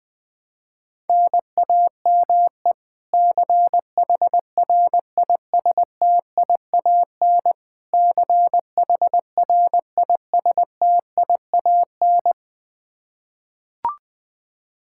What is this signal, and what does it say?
Morse code, heard as a keyed tone, about 20 wpm, 700 Hz. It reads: NAME CHRISTIAN CHRISTIAN
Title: Practice QSO - Exchange 1 - Beginning - 20wpm